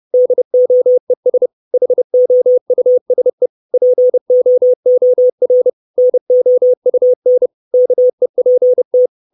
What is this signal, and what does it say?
Morse code, heard as a keyed tone, about 30 wpm, 500 Hz. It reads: DOES HOUSE POOR NOUN KEPT
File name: 632 Words 30WPM